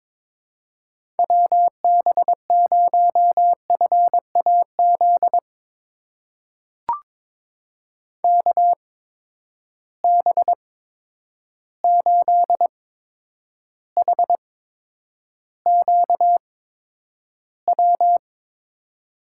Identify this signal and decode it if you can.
Morse code, heard as a keyed tone, about 22 wpm, spaced as for 9 wpm, 700 Hz. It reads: WB0FAZ KB8HQW
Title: USA Call Signs - 6 Characters - 8x Character Spacing - 22wpm